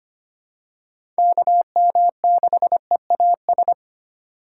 Morse code, heard as a keyed tone, about 25 wpm, 700 Hz. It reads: KM6EAS
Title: USA Call Signs - 6 Characters - 3x Character Spacing - 25wpm